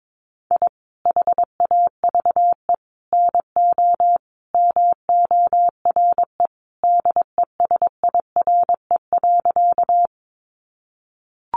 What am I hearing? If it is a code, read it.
Morse code, heard as a keyed tone, about 22 wpm, 700 Hz. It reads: I HAVE NO MORE DESIRE.
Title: Sentences from Top 1100 Words 22wpm